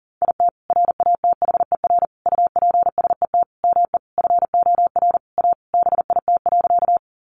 Morse code, heard as a keyed tone, about 40 wpm, 700 Hz. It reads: IT RATHER UPSET ME FOR A BIT.
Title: Sentences from Top 900 Words 40wpm